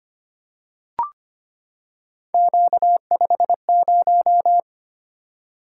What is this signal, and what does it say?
Morse code, heard as a keyed tone, about 25 wpm, 700 Hz. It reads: Q50
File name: Three Letter-Number Combinations - 25wpm